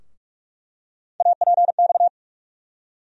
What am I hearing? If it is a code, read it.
Morse code, heard as a keyed tone, about 46 wpm, 700 Hz. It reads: APX
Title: USA Call Sign Postfixes - 3 Letters 45wpm